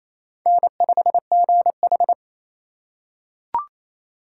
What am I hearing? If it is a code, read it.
Morse code, heard as a keyed tone, about 28 wpm, 700 Hz. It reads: N5GH
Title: USA Call Signs - 4x Character Spacing - 28wpm